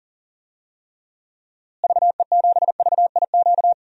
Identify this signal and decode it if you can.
Morse code, heard as a keyed tone, about 40 wpm, 700 Hz. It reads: VE7VIQ
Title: DXCC-100 Call Signs - 6 Characters - Repeated 2x - 40wpm